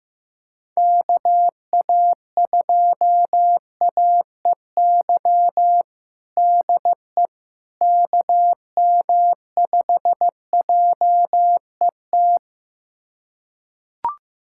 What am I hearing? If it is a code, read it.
Morse code, heard as a keyed tone, about 15 wpm, 700 Hz. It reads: KA2AEY DE KM5JET
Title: USA Callsign DE Callsign - 6 Characters - 15wpm